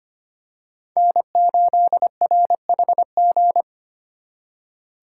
Morse code, heard as a keyed tone, about 25 wpm, 700 Hz. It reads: N8RHG
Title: USA Call Signs - 5 Characters - 4x Character Spacing - 25wpm